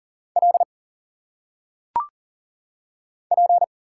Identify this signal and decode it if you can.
Morse code, heard as a keyed tone, about 40 wpm, 700 Hz. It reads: L P
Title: Single Letters - Phonetic Pronunciation 40wpm